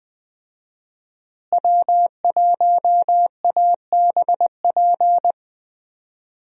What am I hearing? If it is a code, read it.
Morse code, heard as a keyed tone, about 20 wpm, 700 Hz. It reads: W1ABP
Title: USA Call Signs - 5 Characters - Repeated 2x - 20wpm